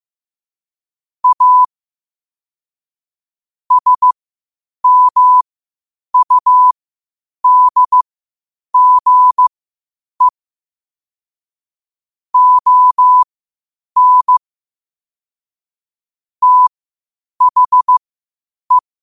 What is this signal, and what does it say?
Morse code, heard as a keyed tone, about 15 wpm, spaced as for 8 wpm, 1000 Hz. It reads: A SMUDGE ON THE
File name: qotdfiles / 8wpm